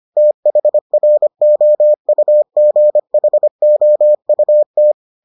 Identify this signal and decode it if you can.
Morse code, heard as a keyed tone, about 25 wpm, 600 Hz. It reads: THROUGHOUT